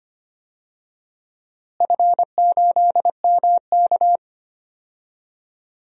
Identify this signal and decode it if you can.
Morse code, heard as a keyed tone, about 25 wpm, 700 Hz. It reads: F8MK
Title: DXCC-100 Call Signs - 4 Characters - Repeated 2x - 25wpm